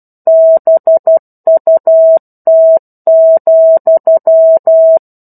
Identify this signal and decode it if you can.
Morse code, heard as a keyed tone, about 12 wpm, 650 Hz. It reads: BUT,
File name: 01-Chapter-01 12 WPM